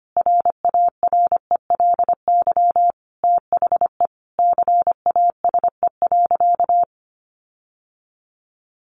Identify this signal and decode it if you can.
Morse code, heard as a keyed tone, about 25 wpm, 700 Hz. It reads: RARELY THE CASE.
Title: Sentences from Top 1400 Words 25wpm